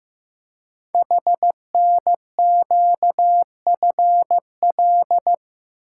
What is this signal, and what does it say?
Morse code, heard as a keyed tone, about 15 wpm, 700 Hz. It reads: HNQFL
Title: Code Groups - 5 Letters - 15wpm